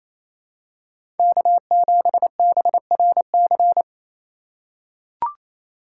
Morse code, heard as a keyed tone, about 27 wpm, 700 Hz. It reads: K7BRC
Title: USA Call Signs - 3x Character Spacing - 28wpm